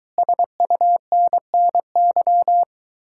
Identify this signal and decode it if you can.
Morse code, heard as a keyed tone, about 23 wpm, 700 Hz. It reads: SUNNY